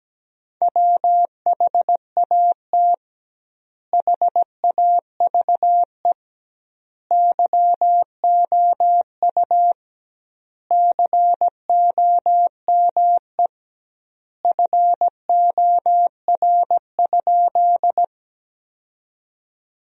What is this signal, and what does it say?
Morse code, heard as a keyed tone, about 17 wpm, 700 Hz. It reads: WHAT HAVE YOU COME FOR?
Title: 2x Word-Spacing - Sentences from Top 100 Words - 17wpm